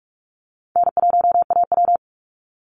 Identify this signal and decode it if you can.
Morse code, heard as a keyed tone, about 44 wpm, 700 Hz. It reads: N1AW
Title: NA SOTA Chaser Callsigns 45wpm